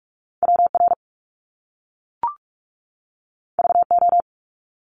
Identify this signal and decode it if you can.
Morse code, heard as a keyed tone, about 46 wpm, 700 Hz. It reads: W R V O
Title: Top Two Letter Combinations in Words 45wpm